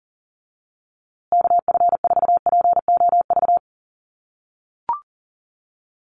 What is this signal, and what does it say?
Morse code, heard as a keyed tone, about 40 wpm, 700 Hz. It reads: KF4POV
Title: USA Call Signs Repeated 3x 40wpm